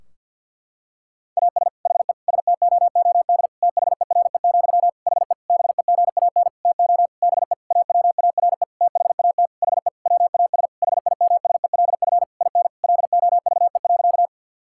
Morse code, heard as a keyed tone, about 50 wpm, 700 Hz. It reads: AS HE STOOD THERE, HE BEGAN TO BE AWARE THAT HE WAS HIMSELF IN LOVE.
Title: Sentences from Top 1600 Words 50wpm